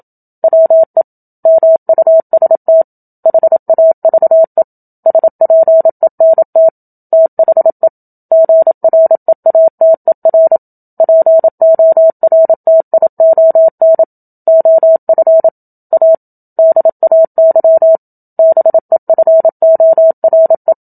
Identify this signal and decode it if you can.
Morse code, heard as a keyed tone, about 27 wpm, 650 Hz. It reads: WE MUST HAVE SPENT THE GREATER PORTION OF A DAY BEFORE
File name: Book Two / 01-Chapter 1 - 27 WPM